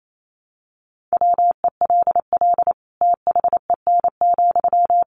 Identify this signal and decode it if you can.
Morse code, heard as a keyed tone, about 28 wpm, 700 Hz. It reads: WELL THEN,
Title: Sentences from Top 200 Words 28wpm